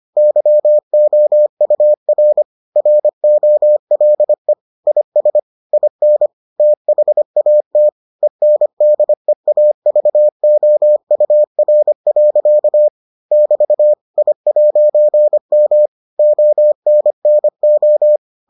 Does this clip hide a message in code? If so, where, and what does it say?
Yes: Morse code from 0.2 to 18.2 s, heard as a keyed tone, about 25 wpm, 600 Hz. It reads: YOUR ROLE IS IN THAT ENDEAVOUR. <BT> I'M ONNO